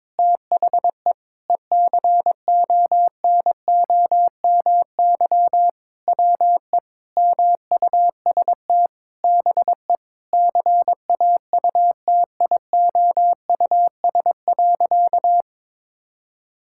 Morse code, heard as a keyed tone, about 22 wpm, 700 Hz. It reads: THE ECONOMY WE MUST BE CAUTIOUS.
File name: practice / 22wpm / Sentences with 8-Word Phrases - 22wpm